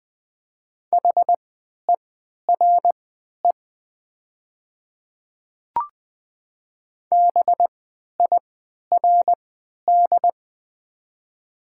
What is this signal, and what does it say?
Morse code, heard as a keyed tone, about 20 wpm, spaced as for 11 wpm, 700 Hz. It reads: HERE BIRD
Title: 3x Character Spacing - Top 500 Words - 20wpm